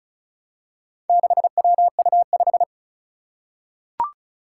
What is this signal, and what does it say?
Morse code, heard as a keyed tone, about 36 wpm, 700 Hz. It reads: 6WU5
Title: Four Challenging Letter-Number Combinations 35wpm